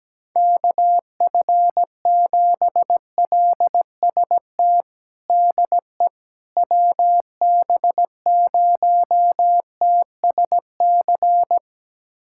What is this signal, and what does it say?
Morse code, heard as a keyed tone, about 17 wpm, 700 Hz. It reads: KF7LST DE WB0TSC
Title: USA Callsign DE Callsign 17wpm